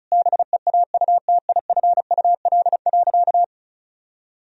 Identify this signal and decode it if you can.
Morse code, heard as a keyed tone, about 36 wpm, 700 Hz. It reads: BEAUTIFUL.
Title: Sentences from Top 1000 Words - Review Part 1 - 35wpm